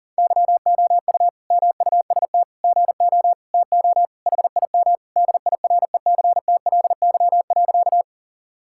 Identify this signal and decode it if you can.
Morse code, heard as a keyed tone, about 40 wpm, 700 Hz. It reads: YOU MUST GO TO HIM DIRECTLY.
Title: Sentences from Top 1000 Words - Review Part 5 - 40wpm